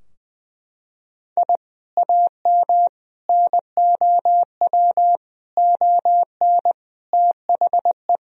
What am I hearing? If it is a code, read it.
Morse code, heard as a keyed tone, about 21 wpm, 700 Hz. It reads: I AM NOW ON THE
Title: Sentences from Top 1000 Words - Review Part 2 - 20wpm